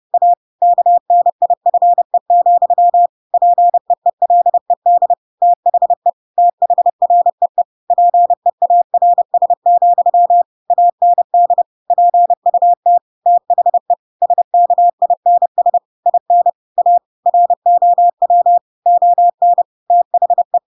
Morse code, heard as a keyed tone, about 30 wpm, 700 Hz. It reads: A KNIFE, PEELED THE THREE PEARS, AND PUT THE SKINS IN A ROW ON THE
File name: Pinocchio / Pinocchio ch7 30 30 ik1hge